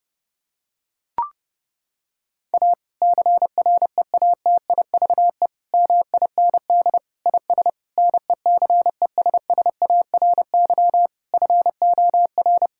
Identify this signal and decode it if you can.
Morse code, heard as a keyed tone, about 32 wpm, 700 Hz. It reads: A CREATIVE MIND IS NECESSARY FOR
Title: Sentences from Top 1000 Words - Review Part 5 - 30wpm